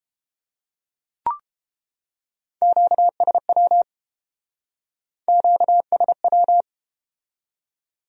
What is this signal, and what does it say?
Morse code, heard as a keyed tone, about 32 wpm, 700 Hz. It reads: QSW QSW